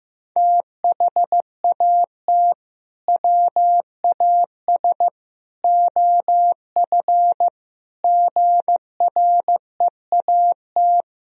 Morse code, heard as a keyed tone, about 15 wpm, 700 Hz. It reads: THAT WAS OF GREAT
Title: Sentences from Top 1000 Words - Review Part 2 - 15wpm